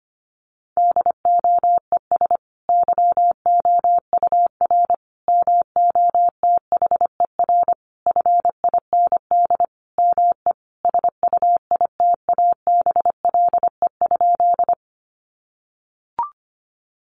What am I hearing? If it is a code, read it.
Morse code, heard as a keyed tone, about 25 wpm, 700 Hz. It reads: DOES YOUR MOTHER FIND ME SUITABLE?
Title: Sentences from Top 1400 Words 25wpm